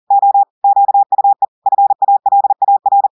Morse code, heard as a keyed tone, about 40 wpm, 800 Hz. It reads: O QUE FALAR
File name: audios / portugais / portugais texte en clair 2 vitesse 40 mots minutes